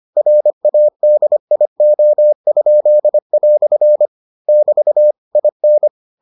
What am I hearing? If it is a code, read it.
Morse code, heard as a keyed tone, about 25 wpm, 600 Hz. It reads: RADIO?" <BT> IN